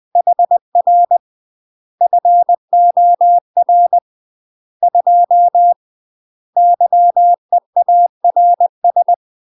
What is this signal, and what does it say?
Morse code, heard as a keyed tone, about 20 wpm, 700 Hz. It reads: HR FOR 2 YEARS
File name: UK MASTER QSOs TRAINING / 20WPM / UK MASTER QSOs 0072